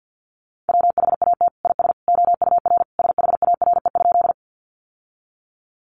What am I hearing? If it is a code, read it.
Morse code, heard as a keyed tone, about 50 wpm, 700 Hz. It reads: WHAT IS OUR SHARE?
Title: Sentences from Top 900 Words 50wpm